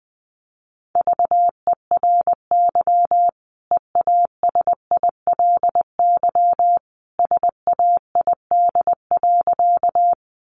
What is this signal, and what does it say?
Morse code, heard as a keyed tone, about 20 wpm, 700 Hz. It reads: VERY EASILY SAID.
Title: Sentences from Top 900 Words 20wpm